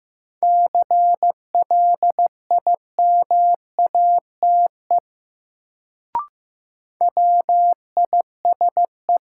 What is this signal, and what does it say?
Morse code, heard as a keyed tone, about 15 wpm, 700 Hz. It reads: CLIMATE WISE
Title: Sentences from Top 1600 Words - New Word Drill - 15wpm